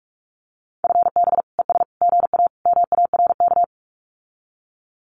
Morse code, heard as a keyed tone, about 46 wpm, 700 Hz. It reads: FB ES GA MARK
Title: Common QSO Phrases Encore 45wpm